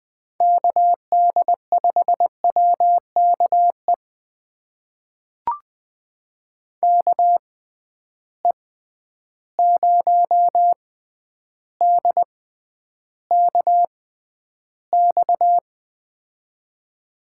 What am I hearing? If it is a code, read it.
Morse code, heard as a keyed tone, about 20 wpm, spaced as for 10 wpm, 700 Hz. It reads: KD5WKE KE0DKX
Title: USA Call Signs - 6 Characters - 6x Character Spacing - 20wpm